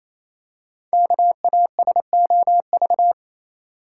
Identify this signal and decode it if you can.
Morse code, heard as a keyed tone, about 28 wpm, 700 Hz. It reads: KASOV
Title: Code Groups - 5 Letters - 28wpm